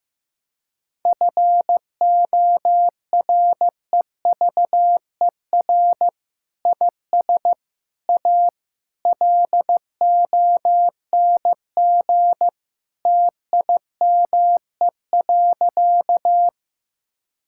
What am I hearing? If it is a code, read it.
Morse code, heard as a keyed tone, about 16 wpm, 700 Hz. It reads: FOREVER IS A LONG TIME.
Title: Sentences from Top 1600 Words 15wpm